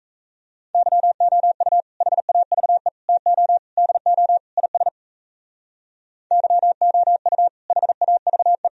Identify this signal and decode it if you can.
Morse code, heard as a keyed tone, about 34 wpm, 700 Hz. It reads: YOU HAVE TO DO IS YOU HAVE
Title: Speed-Racing - Sets of 5 Words 28wpm